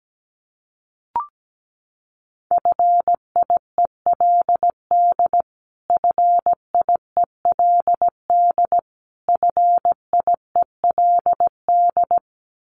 Morse code, heard as a keyed tone, about 17 wpm, 700 Hz. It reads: FIELD FIELD FIELD
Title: Top 500 Words Repeated 17wpm